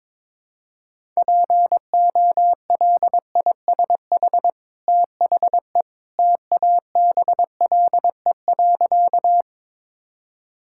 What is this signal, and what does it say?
Morse code, heard as a keyed tone, about 22 wpm, 700 Hz. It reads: POLISH THE TABLE.
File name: practice / 22wpm / Easy Sentences with Hints - 22wpm